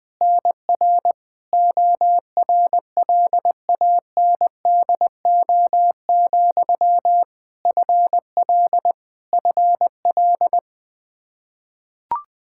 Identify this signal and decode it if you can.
Morse code, heard as a keyed tone, about 20 wpm, 700 Hz. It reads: NR ORLANDO, FL FL
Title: Practice QSO - Exchange 1 - Beginning - 20wpm